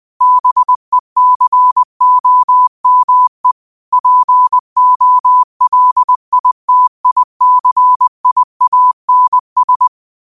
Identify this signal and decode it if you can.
Morse code, heard as a keyed tone, about 20 wpm, 1000 Hz. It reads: BECOME POLITICIANS
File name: qotdfiles / 20wpm